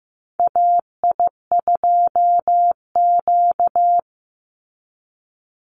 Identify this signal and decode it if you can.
Morse code, heard as a keyed tone, about 15 wpm, 700 Hz. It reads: AI2Q
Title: Call Signs Repeated 2x - 15wpm